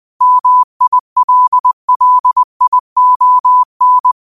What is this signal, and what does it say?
Morse code, heard as a keyed tone, about 20 wpm, 1000 Hz. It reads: MILLION